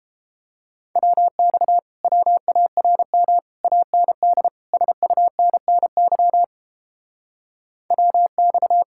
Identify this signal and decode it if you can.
Morse code, heard as a keyed tone, about 29 wpm, 700 Hz. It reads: WX WARM AND SUNNY WX